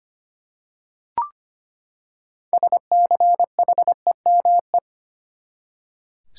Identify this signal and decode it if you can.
Morse code, heard as a keyed tone, about 25 wpm, 700 Hz. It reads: SCHEME
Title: Six Letter Words - 25wpm